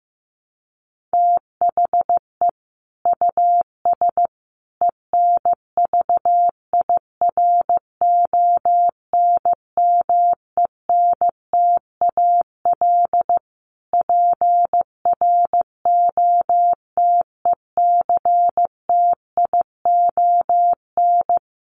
Morse code, heard as a keyed tone, about 15 wpm, 700 Hz. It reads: THE US ENVIRONMENTAL PROTECTION
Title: Sets of 5 Words Encore2 15wpm